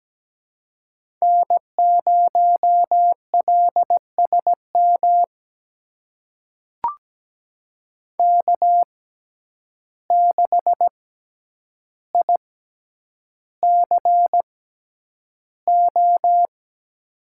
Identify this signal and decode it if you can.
Morse code, heard as a keyed tone, about 17 wpm, spaced as for 9 wpm, 700 Hz. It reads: N0LSM K6ICO
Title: USA Call Signs - 6x Character Spacing - 17wpm